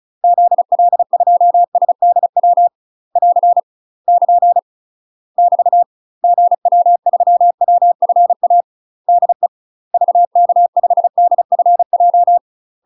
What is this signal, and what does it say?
Morse code, heard as a keyed tone, about 34 wpm, 700 Hz. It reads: ZL2SDW <AR> <KN> <BT> GW3WFA DE VK5DFJ
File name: UK MASTER QSOs TRAINING / 35WPM / UK MASTER QSOs 0015